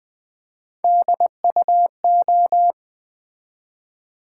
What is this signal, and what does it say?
Morse code, heard as a keyed tone, about 20 wpm, 700 Hz. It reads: DUO
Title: Build A Word - 1 Letter At A Time - 3-Letter Words - 20wpm